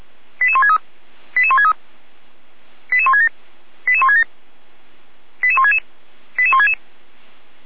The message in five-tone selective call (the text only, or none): 88163 88163 88167 88167 88166 88166